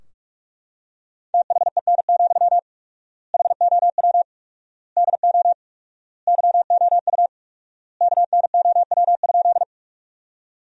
Morse code, heard as a keyed tone, about 43 wpm, 700 Hz. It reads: THEN, HOW DO YOU KNOW?
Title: 4x Word-Spacing - Sentences from Top 100 Words - 45wpm